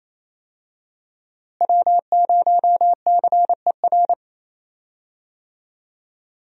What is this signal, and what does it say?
Morse code, heard as a keyed tone, about 27 wpm, 700 Hz. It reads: W0CER
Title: USA Call Signs - 5 Characters - Repeated 3x - 28wpm